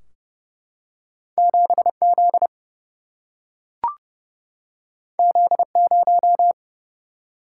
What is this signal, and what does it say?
Morse code, heard as a keyed tone, about 30 wpm, 700 Hz. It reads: 7 Z Z 0
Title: Single-Letter Single-Number 30wpm